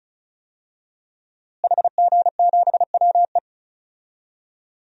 Morse code, heard as a keyed tone, about 34 wpm, 700 Hz. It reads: HG7WE
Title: DXCC-100 Call Signs - 5 Characters - Repeated 2x - 35wpm